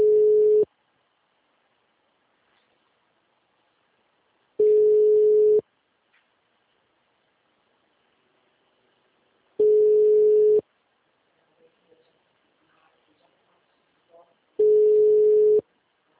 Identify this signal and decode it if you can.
Ringback tone (Europe, 425 Hz), with speech.